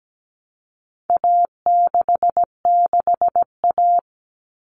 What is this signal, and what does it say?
Morse code, heard as a keyed tone, about 17 wpm, 700 Hz. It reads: A66A
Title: All Cut Numbers - 4-Digit 17wpm